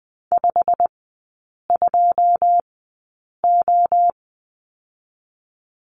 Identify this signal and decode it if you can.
Morse code, heard as a keyed tone, about 20 wpm, 700 Hz. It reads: 5 2 O
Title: 121 Lesson 20e - Number 3 - Three Characters 20wpm